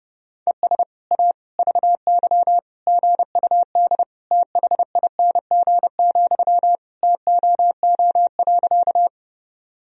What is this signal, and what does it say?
Morse code, heard as a keyed tone, about 30 wpm, 700 Hz. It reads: ES A VY GUD THING, TOO.
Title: Sentences from Top 200 Words and Abbreviations 30wpm